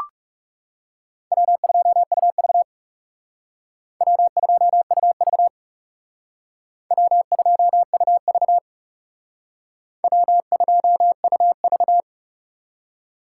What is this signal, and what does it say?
Morse code, heard as a keyed tone, about 37 wpm, 700 Hz. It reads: W2UV W2UV W2UV W2UV
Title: Speed Racing - USA 1x2 Call Signs 30wpm